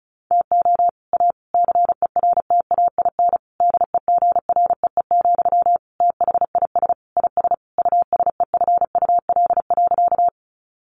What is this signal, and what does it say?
Morse code, heard as a keyed tone, about 34 wpm, 700 Hz. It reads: TO A CERTAIN DEGREE, THIS IS USEFUL.
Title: Sentences from Top 1500 Words 35wpm